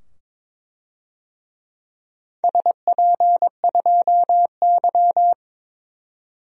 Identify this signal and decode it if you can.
Morse code, heard as a keyed tone, about 22 wpm, 700 Hz. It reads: SP2Y
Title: DXCC-100 Call Signs - 4 Characters - 22wpm